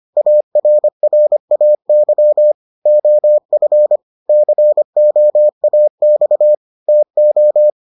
Morse code, heard as a keyed tone, about 25 wpm, 600 Hz. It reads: ARRAY OF COAX TO